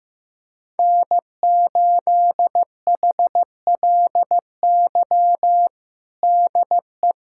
Morse code, heard as a keyed tone, about 15 wpm, 700 Hz. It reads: N8HLY DE
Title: USA Callsign DE Callsign - 5 Characters - 15wpm